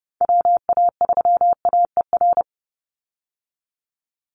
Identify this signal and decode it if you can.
Morse code, heard as a keyed tone, about 30 wpm, 700 Hz. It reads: WA3AER
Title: Call Signs Repeated 2x - 30wpm